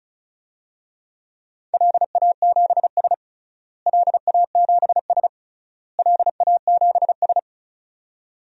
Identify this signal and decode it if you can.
Morse code, heard as a keyed tone, about 36 wpm, 700 Hz. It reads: LA7S LA7S LA7S
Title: DXCC-100 Call Signs - 4 Characters - Repeated 3x - 35wpm